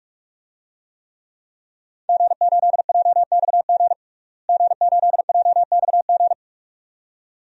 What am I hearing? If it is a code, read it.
Morse code, heard as a keyed tone, about 48 wpm, 700 Hz. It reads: G8JXG G8JXG
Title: DXCC-100 Call Signs - 5 Characters - Repeated 2x - 45wpm